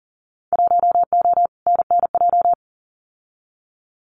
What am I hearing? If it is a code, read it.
Morse code, heard as a keyed tone, about 41 wpm, 700 Hz. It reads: 1O NNJ
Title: Winter Field Day Exchanges 40wpm